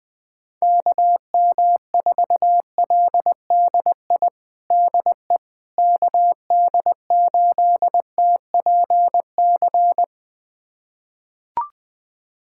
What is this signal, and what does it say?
Morse code, heard as a keyed tone, about 20 wpm, 700 Hz. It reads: KM4LDI DE KD8TPC
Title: USA Callsign DE Callsign - 6 Characters - 20wpm